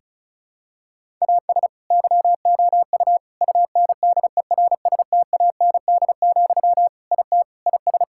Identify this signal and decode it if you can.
Morse code, heard as a keyed tone, about 34 wpm, 700 Hz. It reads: AS YOU UNDERSTAND, IT IS
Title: Sentences from Top 1000 Words - Review Part 5 - 35wpm